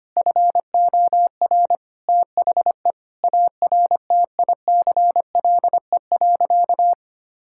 Morse code, heard as a keyed tone, about 25 wpm, 700 Hz. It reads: FOR THE ARTICLE.
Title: Sentences from Top 1000 Words - Review Part 5 - 25wpm